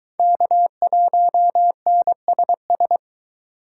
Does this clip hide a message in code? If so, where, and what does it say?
Yes: Morse code from 0.2 to 3.0 s, heard as a keyed tone, about 22 wpm, 700 Hz. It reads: K1NSS